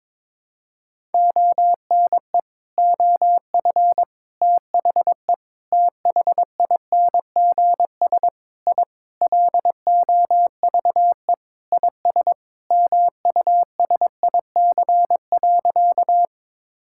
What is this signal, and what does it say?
Morse code, heard as a keyed tone, about 22 wpm, 700 Hz. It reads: ONE OF THE THINGS I LOVE IS MUSIC.
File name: practice / 22wpm / Sentences with 4-Word Phrases - 22wpm